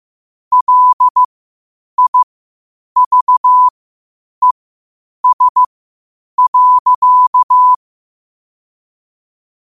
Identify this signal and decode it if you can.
Morse code, heard as a keyed tone, about 15 wpm, spaced as for 8 wpm, 1000 Hz. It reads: LIVES.
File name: qotdfiles / 8wpm